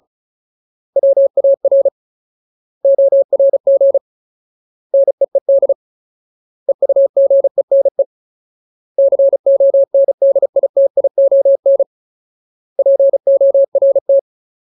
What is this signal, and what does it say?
Morse code, heard as a keyed tone, about 34 wpm, 550 Hz. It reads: WAR ORG NEED EUGENE CONDITION PORT
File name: Ultimate Word List / 35WPM Ultimate